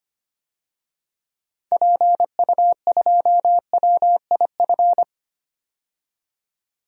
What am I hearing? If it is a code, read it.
Morse code, heard as a keyed tone, about 25 wpm, 700 Hz. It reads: PU2WIF
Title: DXCC-100 Call Signs - 6 Characters - 25wpm